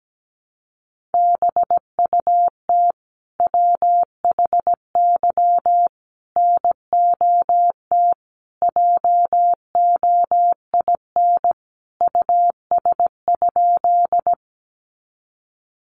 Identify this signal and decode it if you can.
Morse code, heard as a keyed tone, about 17 wpm, 700 Hz. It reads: BUT WHY NOT JOIN US?
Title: Sentences from Top 1400 Words 17wpm